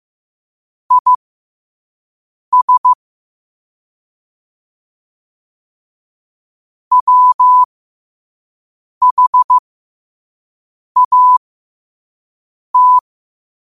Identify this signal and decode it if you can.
Morse code, heard as a keyed tone, about 15 wpm, spaced as for 5 wpm, 1000 Hz. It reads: IS WHAT
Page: Quotes for Tue, 12 Aug 2025 in Morse Code at 5 words per minute.